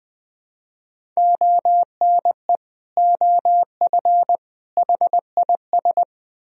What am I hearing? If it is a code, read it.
Morse code, heard as a keyed tone, about 20 wpm, 700 Hz. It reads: ONE OF HIS
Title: Speed-Racing - Sets of 3 Words 20wpm